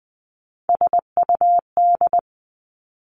Morse code, heard as a keyed tone, about 19 wpm, 700 Hz. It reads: SUD
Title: Three Challenging Letter-Number Combinations 20wpm